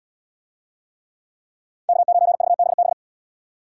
Morse code, heard as a keyed tone, about 70 wpm, 700 Hz. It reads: K9VBB